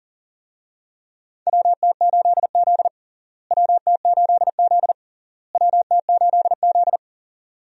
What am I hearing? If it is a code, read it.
Morse code, heard as a keyed tone, about 40 wpm, 700 Hz. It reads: WT8Z WT8Z WT8Z
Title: USA Call Signs - 4 Characters - Repeated 3x - 40wpm